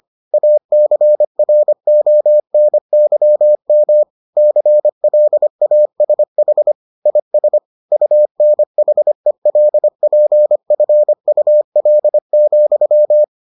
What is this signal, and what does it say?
Morse code, heard as a keyed tone, about 25 wpm, 600 Hz. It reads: ACRONYM CLASH IS UNHELPFUL,